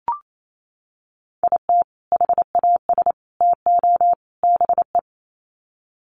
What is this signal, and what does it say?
Morse code, heard as a keyed tone, about 28 wpm, 700 Hz. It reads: IT HAS TO BE
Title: Sets of 4 Words 28wpm